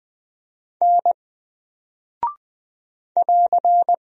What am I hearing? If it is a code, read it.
Morse code, heard as a keyed tone, about 20 wpm, 700 Hz. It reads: N <AR>
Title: Single Character (Letter:Number:Punctuation:Prosign) 20wpm